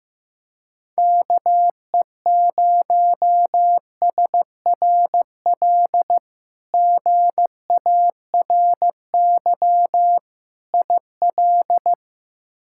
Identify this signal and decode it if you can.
Morse code, heard as a keyed tone, about 15 wpm, 700 Hz. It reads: KE0SRL GARY IL
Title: K1USN SST QSO Practice - 15wpm